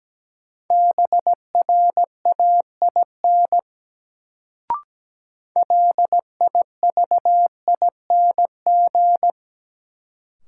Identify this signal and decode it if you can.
Morse code, heard as a keyed tone, about 17 wpm, 700 Hz. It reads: BRAIN LIVING
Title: Sentences from Top 1000 Words - Review Part 6 - New Word Drill - 17wpm